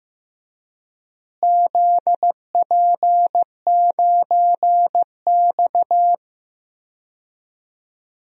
Morse code, heard as a keyed tone, about 15 wpm, 700 Hz. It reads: ZP9X
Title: DXCC-100 Call Signs - 4 Characters - Repeated 3x - 15wpm